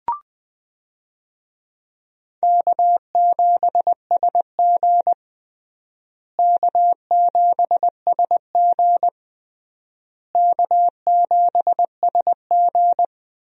Morse code, heard as a keyed tone, about 20 wpm, 700 Hz. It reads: K7SG K7SG K7SG
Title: USA Call Signs - 4 Characters - Repeated 3x - 20wpm